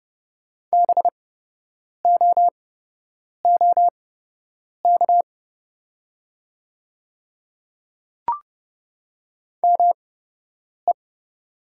Morse code, heard as a keyed tone, about 30 wpm, spaced as for 8 wpm, 700 Hz. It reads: BOOK ME
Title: 8x Character Spacing - Top 500 Words - 30wpm